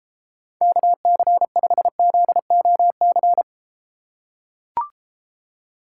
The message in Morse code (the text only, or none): KC5ZOC